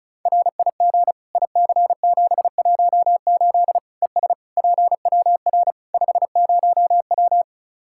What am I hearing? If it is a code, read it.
Morse code, heard as a keyed tone, about 34 wpm, 700 Hz. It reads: RIG IC718 ES PWR 50W
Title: Practice QSO - Exchange 2 - Info - 35wpm